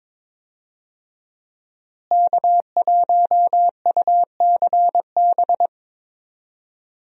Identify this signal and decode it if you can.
Morse code, heard as a keyed tone, about 22 wpm, 700 Hz. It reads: K1UCB
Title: USA Call Signs - 5 Characters - Repeated 2x - 22wpm